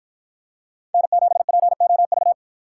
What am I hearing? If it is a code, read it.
Morse code, heard as a keyed tone, about 53 wpm, 700 Hz. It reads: N7POV